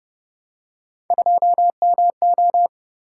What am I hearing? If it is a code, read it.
Morse code, heard as a keyed tone, about 33 wpm, 700 Hz. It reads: 2MO